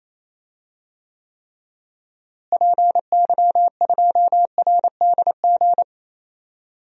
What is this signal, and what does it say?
Morse code, heard as a keyed tone, about 28 wpm, 700 Hz. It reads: PY2RDG
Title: DXCC-100 Call Signs - 6 Characters - Repeated 3x - 28wpm